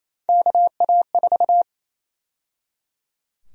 Morse code, heard as a keyed tone, about 28 wpm, 700 Hz. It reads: KA4
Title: USA Call Sign Prefixes - 28wpm